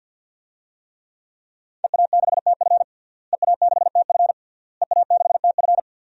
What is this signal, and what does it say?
Morse code, heard as a keyed tone, about 46 wpm, 700 Hz. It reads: EA6TF EA6TF EA6TF
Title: DXCC-100 Call Signs - 5 Characters - Repeated 3x - 50wpm